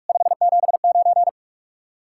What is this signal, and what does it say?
Morse code, heard as a keyed tone, about 54 wpm, 700 Hz. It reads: 579
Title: Speed-Racing Common QSO Phrases - 45wpm